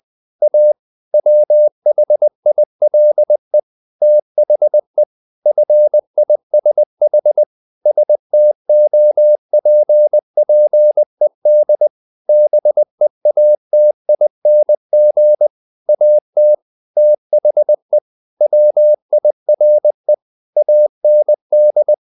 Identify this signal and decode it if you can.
Morse code, heard as a keyed tone, about 20 wpm, 600 Hz. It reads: A WHILE THE FISH STOPPED BEATING AT THE WIRE AND
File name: the old man and the sea part 2 20wpm